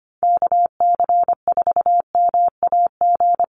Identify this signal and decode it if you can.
Morse code, heard as a keyed tone, about 25 wpm, 700 Hz. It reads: KC4MAG
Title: USA Call Signs Repeated 3x 25wpm